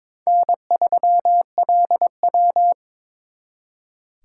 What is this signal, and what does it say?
Morse code, heard as a keyed tone, about 26 wpm, 700 Hz. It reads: N3LW
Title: Speed Racing - USA 1x2 Call Signs 22wpm